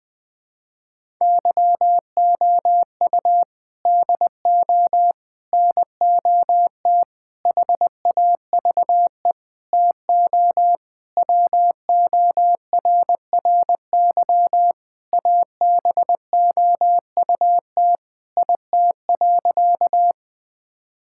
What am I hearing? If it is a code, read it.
Morse code, heard as a keyed tone, about 20 wpm, 700 Hz. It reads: YOU DO NOT HAVE TO WORRY ABOUT IT.
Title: Sentences with 5-Word Phrases - Spoken Beginning - 20wpm